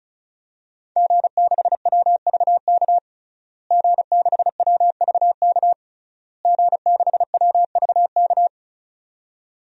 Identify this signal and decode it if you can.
Morse code, heard as a keyed tone, about 36 wpm, 700 Hz. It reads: G6WVK G6WVK G6WVK
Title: DXCC-100 Call Signs - 5 Characters - Repeated 3x - 35wpm